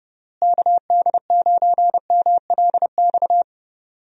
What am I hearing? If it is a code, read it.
Morse code, heard as a keyed tone, about 30 wpm, 700 Hz. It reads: KD9MLX
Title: USA Call Signs - 6 Characters - Repeated 3x - 30wpm